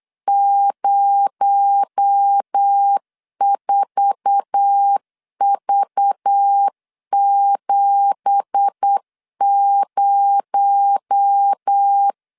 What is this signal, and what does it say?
Morse code, heard as a keyed tone, about 8 wpm, 800 Hz. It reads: 04V70
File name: CW / DET / 1 9 1500C S C 8wpm 5653